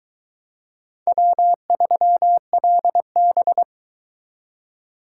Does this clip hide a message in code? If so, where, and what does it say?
Yes: Morse code from 1.1 to 3.6 s, heard as a keyed tone, about 22 wpm, 700 Hz. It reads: W3LB